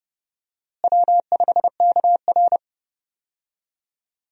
Morse code, heard as a keyed tone, about 30 wpm, 700 Hz. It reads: W5KR
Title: USA Call Signs - 4 Characters - Repeated 2x - 30wpm